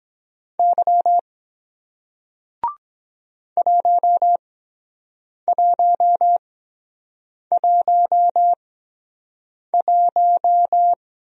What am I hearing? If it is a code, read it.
Morse code, heard as a keyed tone, about 23 wpm, spaced as for 7 wpm, 700 Hz. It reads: Y 1111